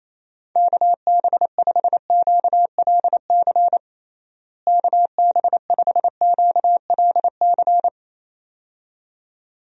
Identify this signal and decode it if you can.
Morse code, heard as a keyed tone, about 27 wpm, 700 Hz. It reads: KB5QLC KB5QLC
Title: USA Call Signs - 6 Characters - Repeated 2x - 28wpm